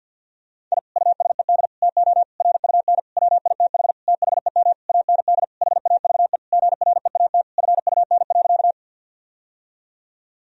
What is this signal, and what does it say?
Morse code, heard as a keyed tone, about 49 wpm, 700 Hz. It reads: I USED TO RUN WITH THEM AND HAVE GREAT FUN.
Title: Sentences from Top 1000 Words - Review Part 4 - 50wpm